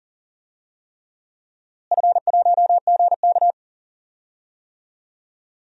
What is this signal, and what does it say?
Morse code, heard as a keyed tone, about 40 wpm, 700 Hz. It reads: F1GK
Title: DXCC-100 Call Signs - 4 Characters - Repeated 3x - 40wpm